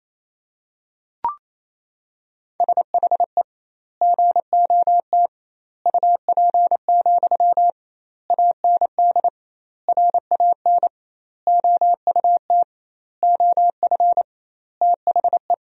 Morse code, heard as a keyed tone, about 28 wpm, 700 Hz. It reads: SHE GOT UP, AND RAN OUT OF THE
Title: 2x Word-Spacing - Sentences from Top 500 Words - 28wpm